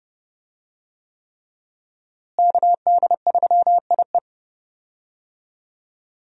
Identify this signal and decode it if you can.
Morse code, heard as a keyed tone, about 30 wpm, 700 Hz. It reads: KD3IE
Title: USA Call Signs - 5 Characters - Repeated 3x - 30wpm